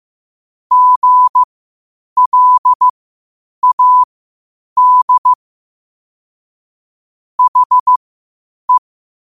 Morse code, heard as a keyed tone, about 15 wpm, spaced as for 8 wpm, 1000 Hz. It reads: GLAD HE
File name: qotdfiles / 8wpm